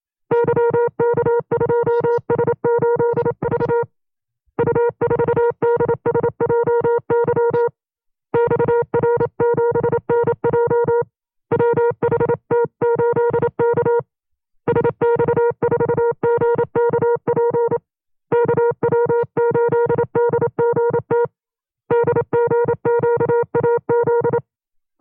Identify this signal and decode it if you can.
Morse code, heard as a keyed tone, about 27 wpm, 470 Hz. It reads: YK2S8V U4DSJY XR7NJ WHT8K SX4GKP KW8DGT DGQAZ
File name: FreeFiles / CW-practice / 25WPM-side1